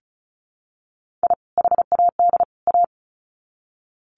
Morse code, heard as a keyed tone, about 34 wpm, 700 Hz. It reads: I HAD A
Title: Sets of 3 Words 35wpm